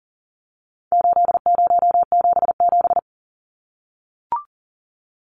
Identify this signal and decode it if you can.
Morse code, heard as a keyed tone, about 39 wpm, 700 Hz. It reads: 8077
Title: Numbers - 4-Digit 40wpm